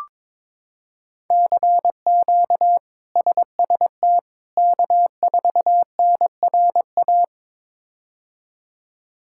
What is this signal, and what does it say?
Morse code, heard as a keyed tone, about 22 wpm, 700 Hz. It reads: CQ SST K4NRA
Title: K1USN SST QSO Practice - 22wpm